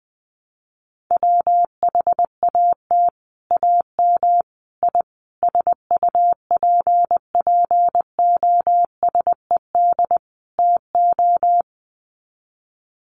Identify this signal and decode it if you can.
Morse code, heard as a keyed tone, about 20 wpm, 700 Hz. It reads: WHAT AM I SUPPOSED TO
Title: Sets of 5 Words Encore2 20wpm